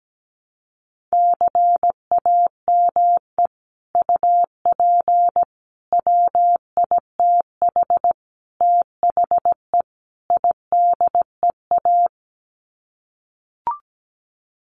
Morse code, heard as a keyed tone, about 17 wpm, 700 Hz. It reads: CAME UP WITH THE IDEA
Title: Sets of 5 Words Encore2 17wpm